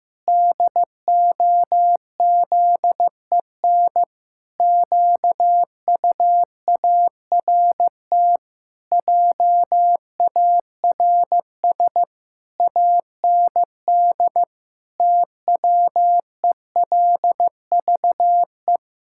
Morse code, heard as a keyed tone, about 15 wpm, 700 Hz. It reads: DOZEN QUART JARS AND TWELVE
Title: Pangram Challenge v2 - 15wpm